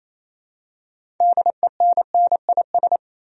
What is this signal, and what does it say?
Morse code, heard as a keyed tone, about 27 wpm, 700 Hz. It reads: DENNIS
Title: USA Common Names - 28wpm